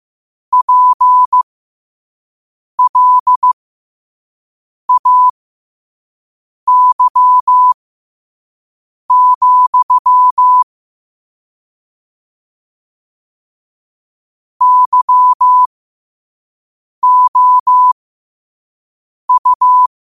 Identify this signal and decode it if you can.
Morse code, heard as a keyed tone, about 15 wpm, spaced as for 5 wpm, 1000 Hz. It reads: PLAY, YOU
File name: 5wpm